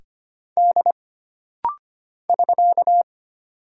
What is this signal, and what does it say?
Morse code, heard as a keyed tone, about 25 wpm, 700 Hz. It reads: D <SK>
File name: Single Character (Letter:Number:Punctuation:Prosign) - Rapid-Fire - 25wpm